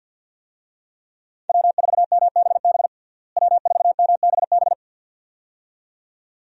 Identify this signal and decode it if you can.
Morse code, heard as a keyed tone, about 50 wpm, 700 Hz. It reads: W4MBB W4MBB
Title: USA Call Signs - 5 Characters - Repeated 2x - 50wpm